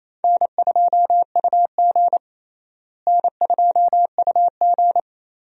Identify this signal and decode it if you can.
Morse code, heard as a keyed tone, about 28 wpm, 700 Hz. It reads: N2UG N2UG
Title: USA Call Signs - 4 Characters - Repeated 3x - 28wpm